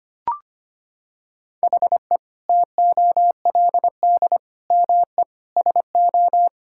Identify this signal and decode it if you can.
Morse code, heard as a keyed tone, about 25 wpm, 700 Hz. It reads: HE TOLD ME SO
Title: Sentences from Top 1600 Words 25wpm